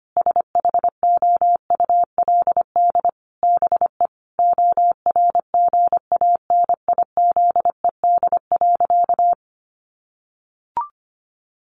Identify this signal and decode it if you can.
Morse code, heard as a keyed tone, about 25 wpm, 700 Hz. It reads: SHOULD BE ORGANIZED.
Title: Sentences from Top 1300 Words 25wpm